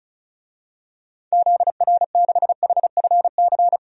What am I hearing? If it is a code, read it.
Morse code, heard as a keyed tone, about 36 wpm, 700 Hz. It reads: ZR6HFC
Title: DXCC-100 Call Signs - 6 Characters - 35wpm